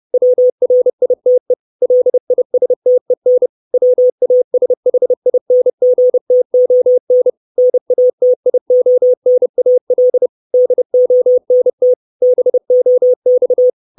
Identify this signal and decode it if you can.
Morse code, heard as a keyed tone, about 30 wpm, 500 Hz. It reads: WRITE LISTEN WASHINGTON NATIONAL DONT BOX
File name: 632 Words 30WPM